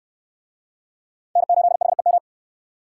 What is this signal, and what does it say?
Morse code, heard as a keyed tone, about 68 wpm, 700 Hz. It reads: N8HEM